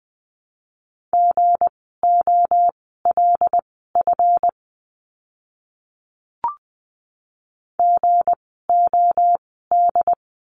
Morse code, heard as a keyed tone, about 20 wpm, spaced as for 15 wpm, 700 Hz. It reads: GOLF GOD
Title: 160 Lesson 26f - Letter g - Words 20wpm